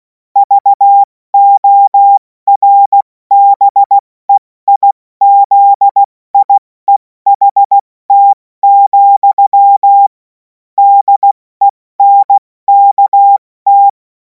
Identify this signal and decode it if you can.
Morse code, heard as a keyed tone, about 16 wpm, 800 Hz. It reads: VORBEIZIEHT, DENKT
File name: Morsekurs / Witze / Witz 05. Mai 2026 70 F80